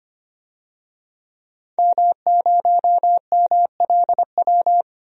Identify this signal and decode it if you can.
Morse code, heard as a keyed tone, about 25 wpm, 690 Hz. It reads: M0MLW
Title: DXCC-100 Call Signs - 5 Characters - 25wpm